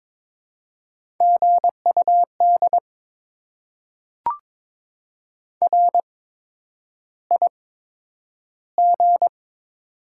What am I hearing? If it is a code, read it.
Morse code, heard as a keyed tone, about 21 wpm, spaced as for 9 wpm, 700 Hz. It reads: GUD RIG
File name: Top 20 QSO Elements - 8x Character Spacing - 22wpm